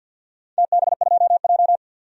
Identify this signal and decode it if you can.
Morse code, heard as a keyed tone, about 49 wpm, 700 Hz. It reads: TB2J
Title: DXCC-100 Call Signs - 4 Characters - 50wpm